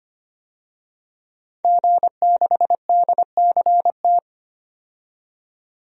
Morse code, heard as a keyed tone, about 25 wpm, 700 Hz. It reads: G6DCT
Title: DXCC-100 Call Signs - 5 Characters - 25wpm